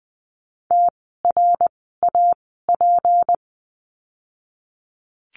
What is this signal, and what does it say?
Morse code, heard as a keyed tone, about 20 wpm, spaced as for 15 wpm, 700 Hz. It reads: TRAP
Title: 153 Lesson 25f - Letter p - Words 20wpm